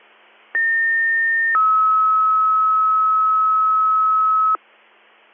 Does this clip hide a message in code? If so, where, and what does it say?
Yes: two-tone page from 0.5 to 4.6 s, heard as a one-second tone, then a three-second tone, about 1800 then 1290 Hz.